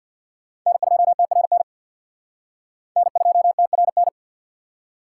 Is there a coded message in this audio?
Yes: Morse code from 0.7 to 4.1 s, heard as a keyed tone, about 65 wpm, 700 Hz. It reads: N2TRN N2TRN